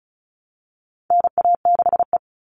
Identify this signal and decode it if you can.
Morse code, heard as a keyed tone, about 34 wpm, 700 Hz. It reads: NA6E
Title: All Cut Numbers - 4-Digit 35wpm